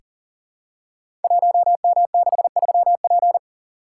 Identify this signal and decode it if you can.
Morse code, heard as a keyed tone, about 39 wpm, 700 Hz. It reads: 1M63P
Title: Code Groups - 5 Letter:Numbers - 40wpm